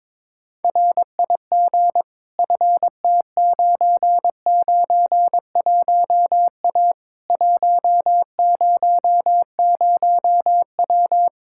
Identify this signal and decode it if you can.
Morse code, heard as a keyed tone, about 22 wpm, 700 Hz. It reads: RIG FT991A 100W
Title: Practice QSO - Exchange 2 - Info - 22wpm